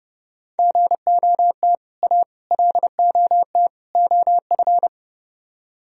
Morse code, heard as a keyed tone, about 28 wpm, 700 Hz. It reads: GOT A LOT OF